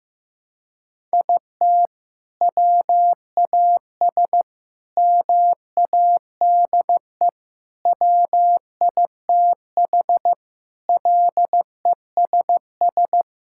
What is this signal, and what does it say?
Morse code, heard as a keyed tone, about 15 wpm, 700 Hz. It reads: IT WAS MADE WITH LESS
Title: Sentences from Top 1000 Words - Review Part 5 - 15wpm